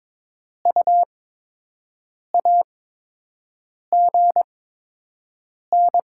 Morse code, heard as a keyed tone, about 22 wpm, spaced as for 6 wpm, 700 Hz. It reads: U AGN
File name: Top 20 QSO Elements - 8x Character Spacing - 22wpm